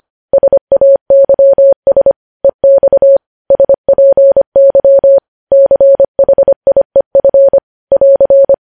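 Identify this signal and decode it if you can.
Morse code, heard as a keyed tone, about 25 wpm, 550 Hz. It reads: SAYS EX SPY CHIEF <AR>
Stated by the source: Morse code news headlines at 25 words per minute.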